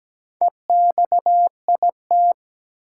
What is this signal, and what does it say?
Morse code, heard as a keyed tone, about 17 wpm, 700 Hz. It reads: EXIT
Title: Sentences from Top 1100 Words - New Word Drill - 17wpm